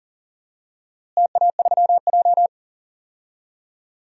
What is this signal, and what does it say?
Morse code, heard as a keyed tone, about 37 wpm, 700 Hz. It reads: TA3J
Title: DXCC-100 Call Signs - 4 Characters - Repeated 3x - 40wpm